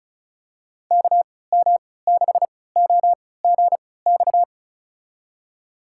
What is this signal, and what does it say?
Morse code, heard as a keyed tone, about 36 wpm, spaced as for 20 wpm, 700 Hz. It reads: KM6OGX
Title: USA Call Signs - 6 Characters - 3x Character Spacing - 35wpm